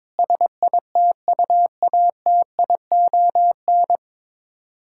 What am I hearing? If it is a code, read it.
Morse code, heard as a keyed tone, about 22 wpm, 700 Hz. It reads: SITUATION
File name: Sets of 9 Words - 22wpm